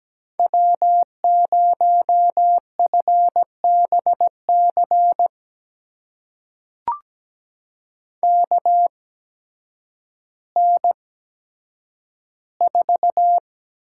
Morse code, heard as a keyed tone, about 17 wpm, spaced as for 9 wpm, 700 Hz. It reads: W0FBC KN4
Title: USA Call Signs - 5 Characters - 8x Character Spacing - 17wpm